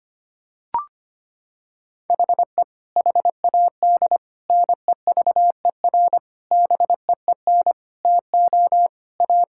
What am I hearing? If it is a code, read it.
Morse code, heard as a keyed tone, about 25 wpm, 700 Hz. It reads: HE HAD NEVER BEEN TO A
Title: Sentences from Top 1000 Words - Review Part 5 - 25wpm